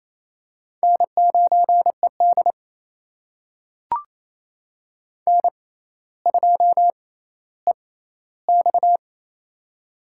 Morse code, heard as a keyed tone, about 28 wpm, spaced as for 14 wpm, 700 Hz. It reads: N9ED N2EX
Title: USA Call Signs - 4 Characters - 6x Character Spacing - 28wpm